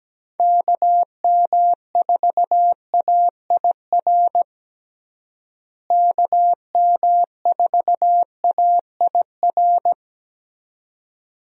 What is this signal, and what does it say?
Morse code, heard as a keyed tone, about 17 wpm, 700 Hz. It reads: KM4AIR KM4AIR
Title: USA Call Signs - 6 Characters - Repeated 2x - 17wpm